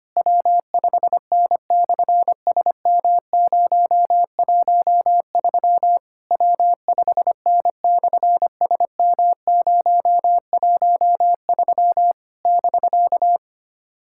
Morse code, heard as a keyed tone, about 24 wpm, 700 Hz. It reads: W5N/SM013 W5N/SM013 <BK>
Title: SOTA QSO Practice 25wpm